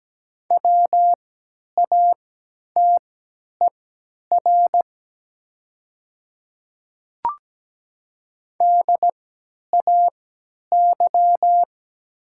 Morse code, heard as a keyed tone, about 17 wpm, spaced as for 10 wpm, 700 Hz. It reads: WATER DAY
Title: 3x Character Spacing - Top 100 Words - 17wpm